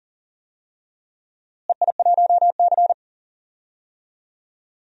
Morse code, heard as a keyed tone, about 40 wpm, 700 Hz. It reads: EI1C
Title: DXCC-100 Call Signs - 4 Characters - Repeated 2x - 40wpm